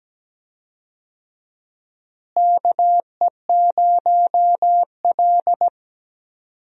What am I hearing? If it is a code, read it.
Morse code, heard as a keyed tone, about 17 wpm, 700 Hz. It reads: KE0L
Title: USA Call Signs - 4 Characters - Repeated 3x - 17wpm